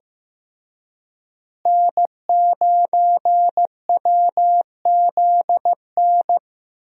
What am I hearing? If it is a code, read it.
Morse code, heard as a keyed tone, about 15 wpm, 700 Hz. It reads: N9WZN
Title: USA Call Signs - 5 Characters - Repeated 3x - 15wpm